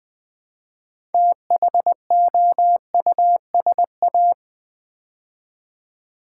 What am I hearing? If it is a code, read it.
Morse code, heard as a keyed tone, about 20 wpm, 700 Hz. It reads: THOUSA
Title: Build A Word - 2 Letters At A Time - 8-Letter Words - 20wpm